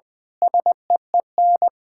Morse code, heard as a keyed tone, about 21 wpm, 700 Hz. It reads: SEEN
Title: Pangram Challenge v2 - 20wpm